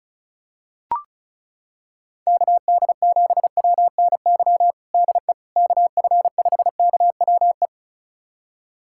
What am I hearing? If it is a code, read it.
Morse code, heard as a keyed tone, about 36 wpm, 700 Hz. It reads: KD7WNY DE KF5KWE
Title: USA Callsign DE Callsign - 6 Characters - 35wpm